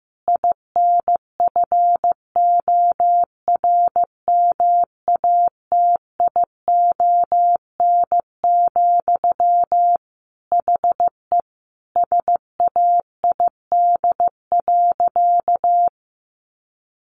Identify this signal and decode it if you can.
Morse code, heard as a keyed tone, about 15 wpm, 700 Hz. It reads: INFORMATION, HE SAID.
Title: Sentences from Top 600 Words 15wpm